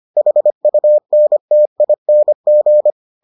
Morse code, heard as a keyed tone, about 25 wpm, 600 Hz. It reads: HUNTING